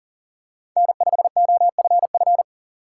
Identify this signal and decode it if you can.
Morse code, heard as a keyed tone, about 40 wpm, 700 Hz. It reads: N5OFF
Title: USA Call Signs - 5 Characters - 4x Character Spacing - 40wpm